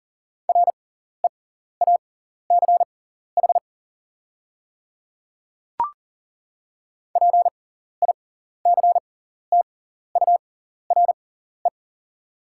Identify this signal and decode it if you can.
Morse code, heard as a keyed tone, about 39 wpm, spaced as for 14 wpm, 700 Hz. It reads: REACH PICTURE
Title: 6x Character Spacing - Top 500 Words - 40wpm